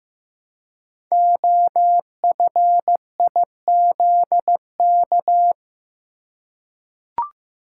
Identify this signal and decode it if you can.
Morse code, heard as a keyed tone, about 15 wpm, 700 Hz. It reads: OFIZK
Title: Code Groups - 5 Letters - 15wpm